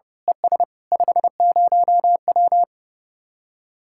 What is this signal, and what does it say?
Morse code, heard as a keyed tone, about 28 wpm, 700 Hz. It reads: ES 50W
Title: Speed-Racing Common QSO Phrases Encore - 30wpm